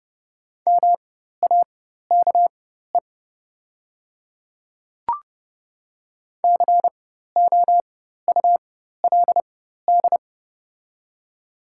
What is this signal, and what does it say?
Morse code, heard as a keyed tone, about 30 wpm, spaced as for 14 wpm, 700 Hz. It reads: MAKE COULD
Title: 4x Character Spacing - Top 100 Words - 30wpm